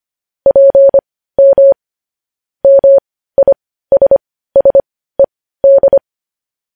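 Morse code, heard as a keyed tone, about 25 wpm, spaced as for 15 wpm, 550 Hz. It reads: PM MISSED
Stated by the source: Morse code practice of news headlines at 15 wpm.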